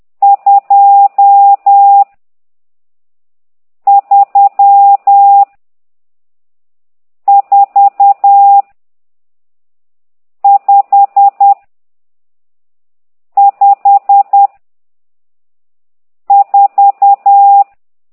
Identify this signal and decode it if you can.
Morse code, heard as a keyed tone, about 10 wpm, 800 Hz. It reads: 2 3 4 5 5 4
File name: ct2idw / exerc12